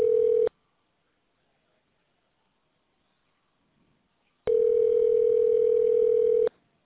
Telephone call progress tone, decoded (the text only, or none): ringback tone